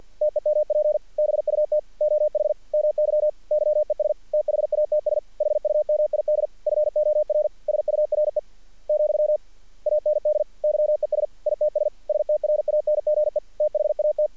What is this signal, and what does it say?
Morse code, heard as a keyed tone, about 48 wpm, 600 Hz. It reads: TEMP BUT OH MY YES THATS HUMID FOR SURE , AND YES ITS STRANGE THAT
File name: archived / CRSnet